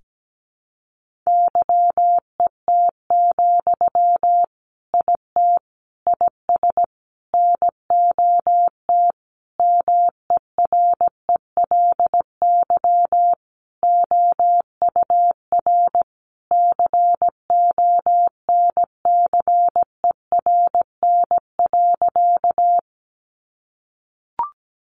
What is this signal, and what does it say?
Morse code, heard as a keyed tone, about 17 wpm, 700 Hz. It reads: YET, IT IS NOT MERELY OUR CONCERN.
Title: Sentences from Top 1400 Words 17wpm